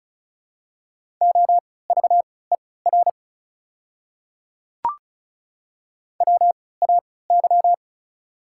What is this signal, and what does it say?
Morse code, heard as a keyed tone, about 33 wpm, spaced as for 20 wpm, 690 Hz. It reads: OVER WAY
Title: 3x Character Spacing - Top 100 Words - 35wpm